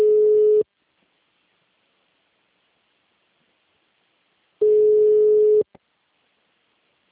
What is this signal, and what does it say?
Ringback tone (Europe, 425 Hz).
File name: out- -2004- - - .0